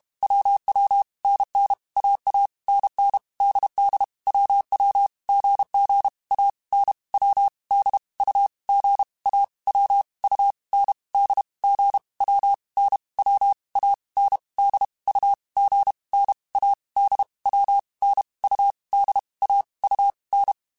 Morse code, heard as a keyed tone, about 32 wpm, 800 Hz. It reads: WW NN AA NN DD WW GG A N W D U G A W U N D G W N W A N D U G N A D W N U D A U N
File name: Basic Sets / Basic Letter Groups Collective7 / AWUNDG@30